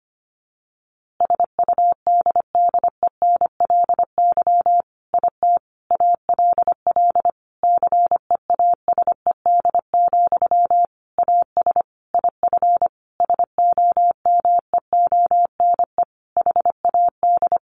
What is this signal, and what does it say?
Morse code, heard as a keyed tone, about 25 wpm, 700 Hz. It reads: SUDDENLY IT ALL CEASED, AS IF SOMEONE HAD
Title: A Sound Of Thunder 25wpm